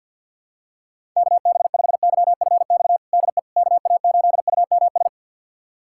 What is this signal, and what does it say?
Morse code, heard as a keyed tone, about 51 wpm, 700 Hz. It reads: KB5YFX DE KA8UMS
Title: USA Callsign DE Callsign - 6 Characters - 50wpm